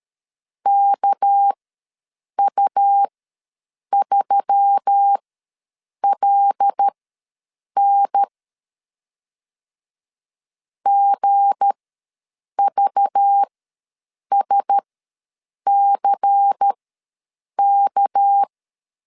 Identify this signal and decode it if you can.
Morse code, heard as a keyed tone, about 13 wpm, spaced as for 7 wpm, 800 Hz. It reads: KU3LN GVSCK
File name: LEZIONI / 7 GF9Z / 7 G ws 12wpm 3140